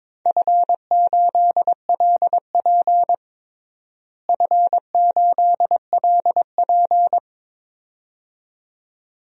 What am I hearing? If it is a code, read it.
Morse code, heard as a keyed tone, about 21 wpm, 700 Hz. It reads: F8LP F8LP
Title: DXCC-100 Call Signs - 4 Characters - Repeated 2x - 22wpm